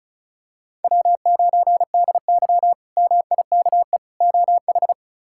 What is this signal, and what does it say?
Morse code, heard as a keyed tone, about 34 wpm, 700 Hz. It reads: W9DY MIKE OH
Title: K1USN SST QSO Practice - 35wpm